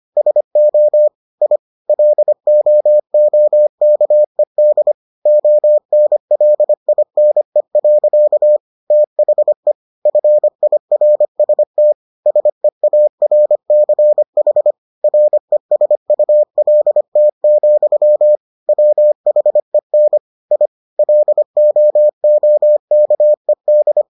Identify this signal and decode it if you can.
Morse code, heard as a keyed tone, about 25 wpm, 600 Hz. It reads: SO I LOOKED ONLINE. THE FIRST SEARCH RESULT, WHEN I LOOKED